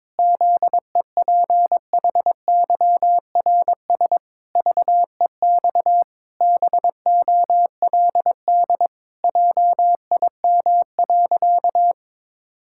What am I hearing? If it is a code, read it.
Morse code, heard as a keyed tone, about 22 wpm, 700 Hz. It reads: ZEPHYRS VEX BOLD JIM.
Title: Pangram Challenge v2 - 22wpm